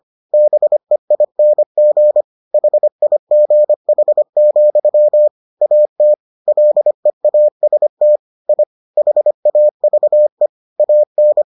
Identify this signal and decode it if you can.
Morse code, heard as a keyed tone, about 25 wpm, 600 Hz. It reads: BEING HIGH, AT LEAST I HAVE AN